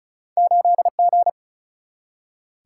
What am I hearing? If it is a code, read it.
Morse code, heard as a keyed tone, about 36 wpm, 700 Hz. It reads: 8G
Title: Single-Letter Single-Number 35wpm